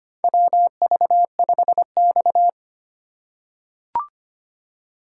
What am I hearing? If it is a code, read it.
Morse code, heard as a keyed tone, about 25 wpm, 700 Hz. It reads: WV5X
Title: USA Call Signs - 4 Characters - 3x Character Spacing - 25wpm